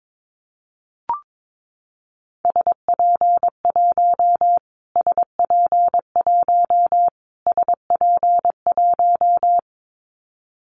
Morse code, heard as a keyed tone, about 22 wpm, 700 Hz. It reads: SP1 SP1 SP1
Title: Callsign Prefixes for Top 25 DXCC - Repeated 3x - 22wpm